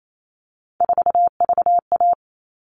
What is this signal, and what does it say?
Morse code, heard as a keyed tone, about 28 wpm, 700 Hz. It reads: 4VA
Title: All Cut Numbers - 3-Digit 28wpm